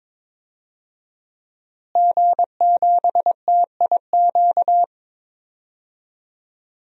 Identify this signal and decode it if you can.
Morse code, heard as a keyed tone, about 23 wpm, 700 Hz. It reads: G7TIQ
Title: DXCC-100 Call Signs - 5 Characters - Repeated 3x - 22wpm